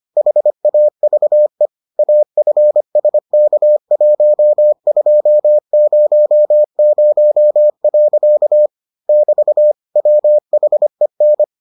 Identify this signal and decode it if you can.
Morse code, heard as a keyed tone, about 25 wpm, 600 Hz. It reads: HAVE AFSK1200. <BT> WHEN